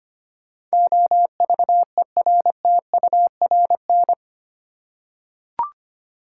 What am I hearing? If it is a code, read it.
Morse code, heard as a keyed tone, about 25 wpm, 690 Hz. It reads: OVERTURN
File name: 8 Letter Words - First 3 Letters Given - 25wpm